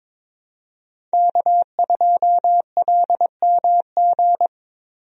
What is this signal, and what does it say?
Morse code, heard as a keyed tone, about 21 wpm, 700 Hz. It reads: K2LMG
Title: USA Call Signs - 5 Characters - Repeated 3x - 22wpm